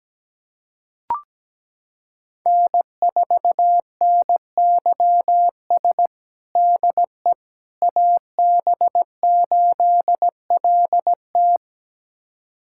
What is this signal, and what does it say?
Morse code, heard as a keyed tone, about 17 wpm, 700 Hz. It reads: N4NYS DE AB8LT
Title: USA Callsign DE Callsign - 5 Characters - 17wpm